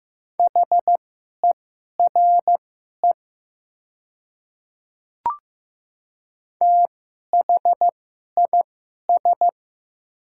Morse code, heard as a keyed tone, about 15 wpm, spaced as for 11 wpm, 700 Hz. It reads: HERE THIS
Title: 2x Character Spacing - Top 200 Words - 15wpm